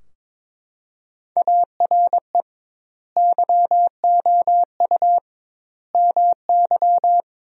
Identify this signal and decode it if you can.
Morse code, heard as a keyed tone, about 21 wpm, 700 Hz. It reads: ARE YOU MY
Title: 2x Word-Spacing - Sentences from Top 300 Words - 22wpm